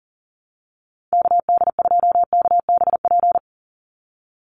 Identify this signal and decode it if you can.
Morse code, heard as a keyed tone, about 40 wpm, 690 Hz. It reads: KD2KBP
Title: USA Call Signs 40wpm